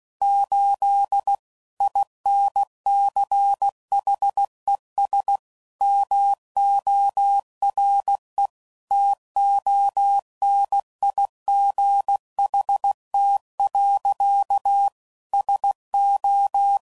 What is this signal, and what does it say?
Morse code, heard as a keyed tone, about 16 wpm, 800 Hz. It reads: 8 INCHES MORE TONIGHT. SO
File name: CW Code Groups @15WPM / QSOs / WA7QOX